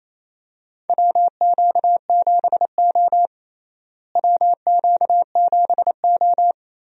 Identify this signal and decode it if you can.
Morse code, heard as a keyed tone, about 28 wpm, 700 Hz. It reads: WQ7O WQ7O
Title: USA Call Signs - 4 Characters - Repeated 2x - 28wpm